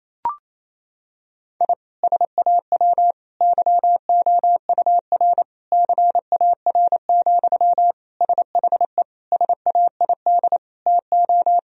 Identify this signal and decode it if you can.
Morse code, heard as a keyed tone, about 28 wpm, 700 Hz. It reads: I SAW YOUR CAR, SHE SAID TO
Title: Sentences from Top 1000 Words - Review Part 1 - 28wpm